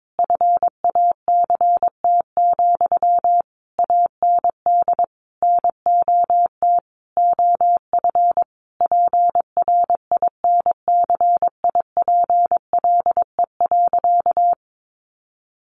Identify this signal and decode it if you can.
Morse code, heard as a keyed tone, about 22 wpm, 700 Hz. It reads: FACT, AND NOT OF PRINCIPLE.
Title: Sentences from Top 1400 Words 22wpm